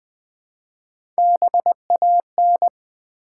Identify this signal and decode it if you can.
Morse code, heard as a keyed tone, about 20 wpm, 700 Hz. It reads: BAN
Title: Build A Word - 1 Letter At A Time - 3-Letter Words - 20wpm